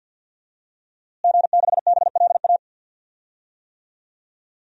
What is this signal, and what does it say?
Morse code, heard as a keyed tone, about 47 wpm, 700 Hz. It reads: G6BLA
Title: DXCC-100 Call Signs - 5 Characters - Repeated 2x - 50wpm